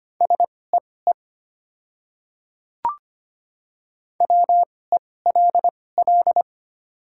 Morse code, heard as a keyed tone, about 25 wpm, spaced as for 18 wpm, 700 Hz. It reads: SEE WELL
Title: 2x Character Spacing - Top 200 Words - 25wpm